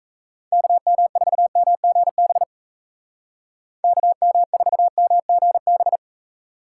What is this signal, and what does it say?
Morse code, heard as a keyed tone, about 36 wpm, 700 Hz. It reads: KM4MGB KM4MGB